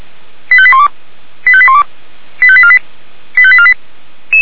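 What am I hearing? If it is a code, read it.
Five-tone selective call (ZVEI), heard as quick runs of five tones, tones of about 70 ms. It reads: 85812 85812 85848 85848
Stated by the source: ZVEI five-tone selective call received by SDR.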